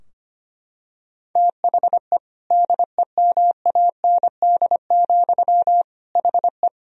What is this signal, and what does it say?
Morse code, heard as a keyed tone, about 25 wpm, 700 Hz. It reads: THE DEMAND, HE
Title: Sentences from Top 1000 Words - Review Part 4 - 25wpm